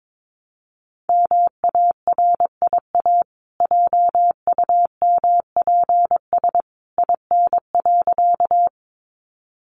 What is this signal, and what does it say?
Morse code, heard as a keyed tone, about 22 wpm, 700 Hz. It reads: MARIA JUMPS IN.
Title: Easy Phrases 22wpm